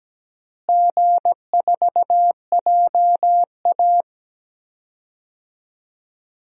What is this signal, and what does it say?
Morse code, heard as a keyed tone, about 17 wpm, 700 Hz. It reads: G4JA
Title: DXCC-100 Call Signs - 4 Characters - Repeated 3x - 17wpm